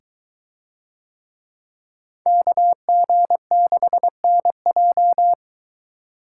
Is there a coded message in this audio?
Yes: Morse code from 2.3 to 5.3 s, heard as a keyed tone, about 23 wpm, 700 Hz. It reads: KG6NJ